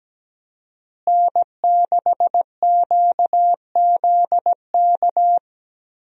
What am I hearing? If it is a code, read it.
Morse code, heard as a keyed tone, about 17 wpm, 700 Hz. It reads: N6QZK
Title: USA Call Signs - 5 Characters - Repeated 3x - 17wpm